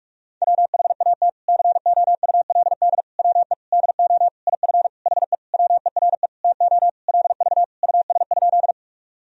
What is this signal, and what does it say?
Morse code, heard as a keyed tone, about 46 wpm, 700 Hz. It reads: WHAT COULD WE DO IF HE WERE TO LV US?
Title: Sentences from Top 400 Words and Abbreviations 45wpm